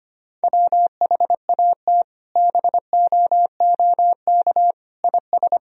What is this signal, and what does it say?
Morse code, heard as a keyed tone, about 25 wpm, 700 Hz. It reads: WHAT BOOK IS
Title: Sentences from Top 1000 Words - Review Part 1 - 25wpm